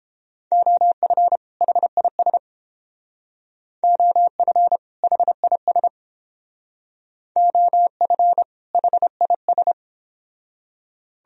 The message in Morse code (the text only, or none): OF HIS OF HIS OF HIS